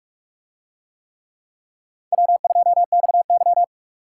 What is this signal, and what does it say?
Morse code, heard as a keyed tone, about 43 wpm, 700 Hz. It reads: W2XY